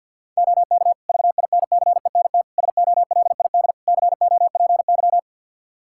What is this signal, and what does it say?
Morse code, heard as a keyed tone, about 50 wpm, 700 Hz. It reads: OK VINCENT SOLID COPY
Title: Practice QSO - Exchange 3 - More Info - 50wpm